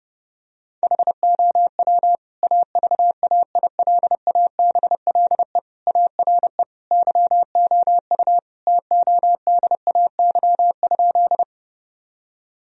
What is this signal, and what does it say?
Morse code, heard as a keyed tone, about 30 wpm, 700 Hz. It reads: HOW AVAILABLE ARE YOU TODAY?
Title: Sentences from Top 1000 Words - Review Part 3 - 30wpm